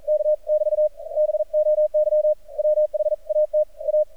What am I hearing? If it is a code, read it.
Morse code, heard as a keyed tone, about 43 wpm, 610 Hz. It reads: K<BT> BOY WHAT A
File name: archived / CRSnet